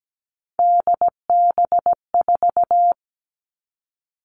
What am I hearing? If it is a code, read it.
Morse code, heard as a keyed tone, about 17 wpm, 700 Hz. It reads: DB4
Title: Three Challenging Letter-Number Combinations 17wpm